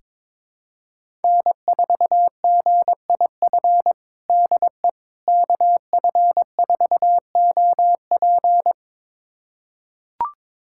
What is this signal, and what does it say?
Morse code, heard as a keyed tone, about 21 wpm, 700 Hz. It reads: N4GIF DE KF4OP
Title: USA Callsign DE Callsign - 5 Characters - 22wpm